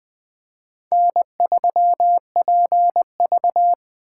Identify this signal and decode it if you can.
Morse code, heard as a keyed tone, about 24 wpm, 700 Hz. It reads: N3PV
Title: Speed Racing - USA Call Signs 20wpm